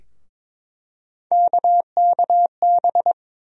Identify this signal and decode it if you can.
Morse code, heard as a keyed tone, about 22 wpm, 700 Hz. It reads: KKB
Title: USA Call Sign Postfixes - 3 Letters 22wpm